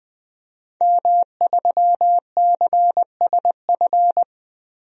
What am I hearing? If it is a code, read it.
Morse code, heard as a keyed tone, about 20 wpm, 700 Hz. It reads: M3CSF
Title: DXCC-100 Call Signs - 5 Characters - 20wpm